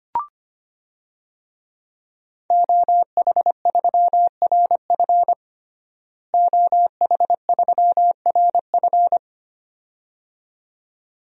Morse code, heard as a keyed tone, about 25 wpm, 700 Hz. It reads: OH3RF OH3RF
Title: DXCC-100 Call Signs - 5 Characters - Repeated 2x - 25wpm